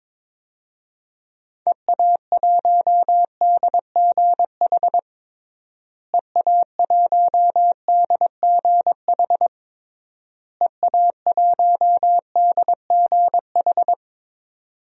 Morse code, heard as a keyed tone, about 22 wpm, 700 Hz. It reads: EA1DGH EA1DGH EA1DGH
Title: DXCC-100 Call Signs - 6 Characters - Repeated 3x - 22wpm